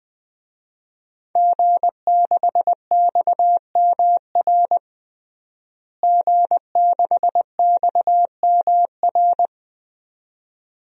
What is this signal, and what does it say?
Morse code, heard as a keyed tone, about 20 wpm, 700 Hz. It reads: G6XMR G6XMR
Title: DXCC-100 Call Signs - 5 Characters - Repeated 2x - 20wpm